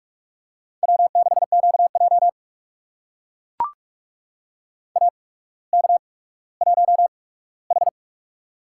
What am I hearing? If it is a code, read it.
Morse code, heard as a keyed tone, about 44 wpm, spaced as for 19 wpm, 700 Hz. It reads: W6QJ AK1H
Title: USA Call Signs - 8x Character Spacing - 45wpm